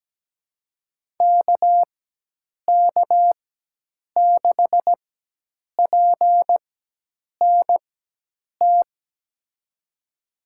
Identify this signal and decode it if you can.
Morse code, heard as a keyed tone, about 17 wpm, spaced as for 8 wpm, 700 Hz. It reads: KK6PNT
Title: USA Call Signs - 6 Characters - 4x Character Spacing - 17wpm